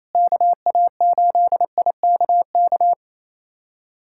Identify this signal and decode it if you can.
Morse code, heard as a keyed tone, about 28 wpm, 700 Hz. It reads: KA8IKK
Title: USA Callsign DE Callsign - 6 Characters - 28wpm